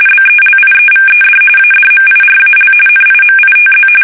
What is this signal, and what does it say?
RTTY, about 45.45 baud. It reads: BELIEVED ADMIRAL NIMITZ
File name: fsk850 / battleshipnj